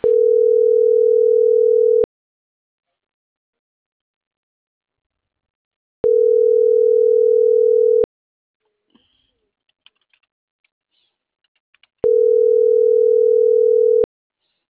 Ringback tone (North America), 440 + 480 Hz.